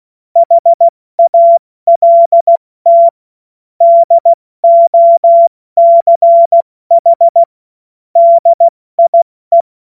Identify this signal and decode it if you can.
Morse code, heard as a keyed tone, about 16 wpm, 680 Hz. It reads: HALT DOCH DIE
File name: Morsekurs / Witze / Witz 24. August 2021, 70 F80